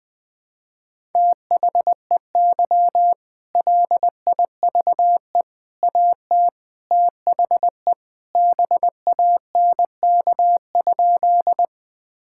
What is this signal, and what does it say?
Morse code, heard as a keyed tone, about 20 wpm, 700 Hz. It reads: THEY LIVE AT THE BANK?
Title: Sentences from Top 1000 Words - Review Part 6 - 20wpm